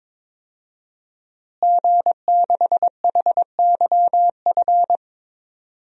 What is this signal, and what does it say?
Morse code, heard as a keyed tone, about 22 wpm, 700 Hz. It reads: G6HYF
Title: DXCC-100 Call Signs - 5 Characters - Repeated 3x - 22wpm